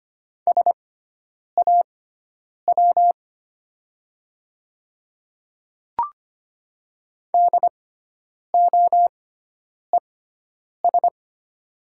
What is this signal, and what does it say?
Morse code, heard as a keyed tone, about 25 wpm, spaced as for 9 wpm, 700 Hz. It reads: SAW DOES
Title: 6x Character Spacing - Top 300 Words - 25wpm